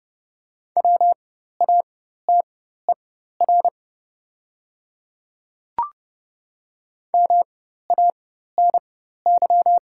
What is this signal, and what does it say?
Morse code, heard as a keyed tone, about 30 wpm, spaced as for 14 wpm, 700 Hz. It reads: WATER MANY
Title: 4x Character Spacing - Top 100 Words - 30wpm